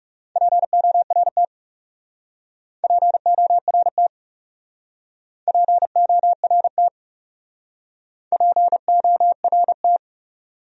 Morse code, heard as a keyed tone, about 39 wpm, 700 Hz. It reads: PORT PORT PORT PORT